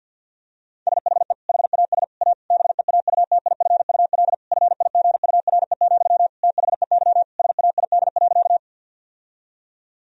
Morse code, heard as a keyed tone, about 50 wpm, 700 Hz. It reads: SHE HAS A BEAUTIFUL FIGURE, THEY SAID.
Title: Sentences from Top 1000 Words - Review Part 2 - 50wpm